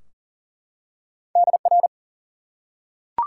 Morse code, heard as a keyed tone, about 40 wpm, 700 Hz. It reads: DR
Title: Top 90 QSO Elements 40wpm